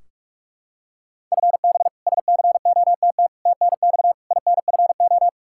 Morse code, heard as a keyed tone, about 46 wpm, 700 Hz. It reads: FB SCOTT TNX INFO
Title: K1USN SST QSO Practice 45wpm